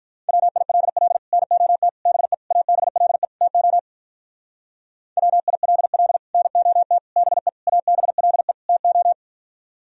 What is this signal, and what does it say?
Morse code, heard as a keyed tone, about 45 wpm, 700 Hz. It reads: WILL NOT BE ABLE TO WILL NOT BE ABLE TO